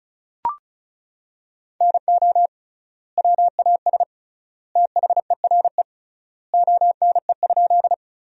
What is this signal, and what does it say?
Morse code, heard as a keyed tone, about 36 wpm, 700 Hz. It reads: NO WAS THERE ONE?
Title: 3x Word-Spacing - Sentences from Top 100 Words - 35wpm